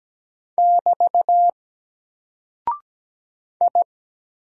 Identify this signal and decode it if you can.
Morse code, heard as a keyed tone, about 17 wpm, 700 Hz. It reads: <BT> I
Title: Single Character (Letter:Number:Punctuation:Prosign) 17wpm